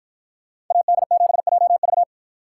Morse code, heard as a keyed tone, about 48 wpm, 700 Hz. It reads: AD7JV